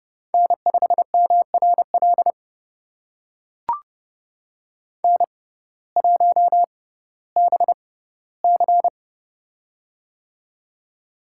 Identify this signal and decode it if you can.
Morse code, heard as a keyed tone, about 30 wpm, spaced as for 17 wpm, 700 Hz. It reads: N5MRL N1BC
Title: USA Call Signs - 6x Character Spacing - 30wpm